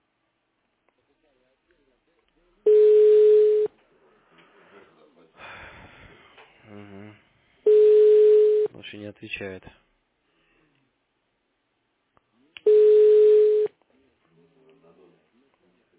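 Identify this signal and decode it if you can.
Ringback tone (Europe, 425 Hz), with speech.
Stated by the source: Call recording from the petrosvyaz mdpbx telephone exchange.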